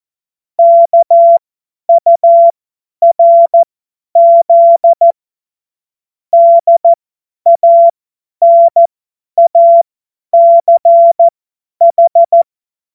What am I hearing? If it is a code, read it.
Morse code, heard as a keyed tone, about 14 wpm, spaced as for 10 wpm, 680 Hz. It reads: KURZ DANACH
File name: Morsekurs / Witze / Witz 31.Oktober 2023 50 F70